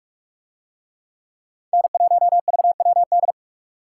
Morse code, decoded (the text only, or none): N1VWD